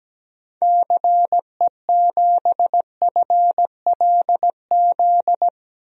Morse code, heard as a keyed tone, about 17 wpm, 700 Hz. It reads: CE7FLZ
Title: DXCC-100 Call Signs - 6 Characters - 17wpm